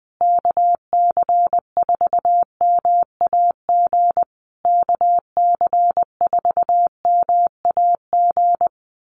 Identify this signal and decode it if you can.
Morse code, heard as a keyed tone, about 20 wpm, 700 Hz. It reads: KC4MAG KC4MAG
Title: USA Call Signs Repeated 3x 20wpm